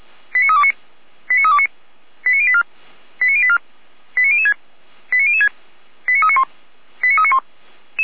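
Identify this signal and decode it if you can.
Five-tone selective call (ZVEI), heard as quick runs of five tones, tones of about 70 ms. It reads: 89329 89329 89084 89084 89006 89006 89391 89391
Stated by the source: ZVEI five-tone call received by SDR.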